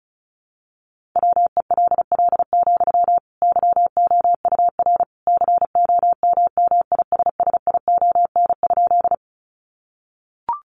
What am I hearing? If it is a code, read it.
Morse code, heard as a keyed tone, about 34 wpm, 700 Hz. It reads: WELL, YOUR COMMISSION?
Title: Sentences from Top 1400 Words 35wpm